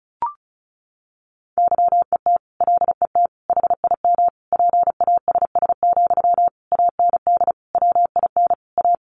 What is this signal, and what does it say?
Morse code, heard as a keyed tone, about 34 wpm, 700 Hz. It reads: YET LET HIM PASS, AND WIN A
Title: Sentences from Top 1500 Words 35wpm